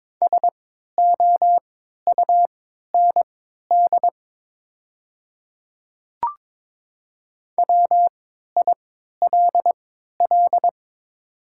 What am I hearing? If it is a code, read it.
Morse code, heard as a keyed tone, about 22 wpm, spaced as for 13 wpm, 700 Hz. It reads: SOUND WILL
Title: 3x Character Spacing - Top 100 Words - 22wpm